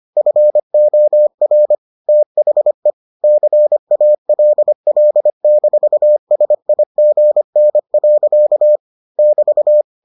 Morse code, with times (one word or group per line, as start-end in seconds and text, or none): 0.2-1.8 FOR
2.1-2.9 THE
3.2-8.8 CALL-SIGN.
9.2-9.8 <BT>